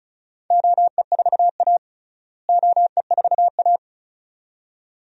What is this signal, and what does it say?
Morse code, heard as a keyed tone, about 34 wpm, 700 Hz. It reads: OE4A OE4A
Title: DXCC-100 Call Signs - 4 Characters - Repeated 2x - 35wpm